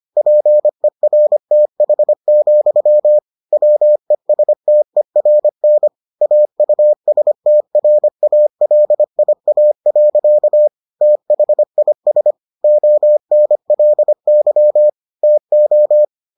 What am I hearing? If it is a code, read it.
Morse code, heard as a keyed tone, about 25 wpm, 600 Hz. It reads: PERTH, WESTERN AUSTRALIA. THIS ONLY TO